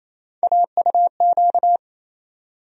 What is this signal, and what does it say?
Morse code, heard as a keyed tone, about 28 wpm, 700 Hz. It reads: AUQ
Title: USA Call Sign Postfixes - 3 Letters 28wpm